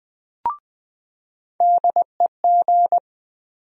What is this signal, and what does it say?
Morse code, heard as a keyed tone, about 20 wpm, 700 Hz. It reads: DEG
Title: USA Call Sign Postfixes - 3 Letters 20wpm